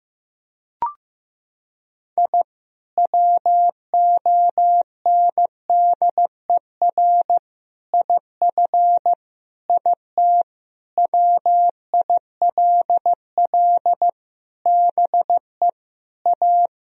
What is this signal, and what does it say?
Morse code, heard as a keyed tone, about 16 wpm, 700 Hz. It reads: I WONDER IF IT WILL BE A
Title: Sentences from Top 1000 Words - Review Part 3 - 15wpm